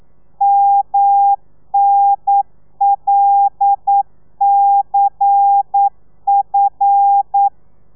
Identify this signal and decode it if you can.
Morse code, heard as a keyed tone, about 9 wpm, 800 Hz. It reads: MNLCF
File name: telegrafia / i2tfj / Lez11 40